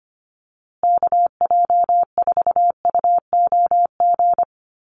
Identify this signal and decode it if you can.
Morse code, heard as a keyed tone, about 25 wpm, 700 Hz. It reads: KJ4UOG
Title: USA Call Signs Repeated 3x 25wpm